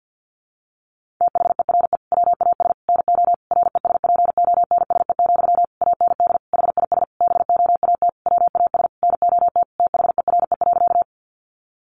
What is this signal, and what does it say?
Morse code, heard as a keyed tone, about 48 wpm, 700 Hz. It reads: THERE WAS NO RESPONSE, AND HIS BOAT WAS NOT THERE.
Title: Sentences from Top 1400 Words 50wpm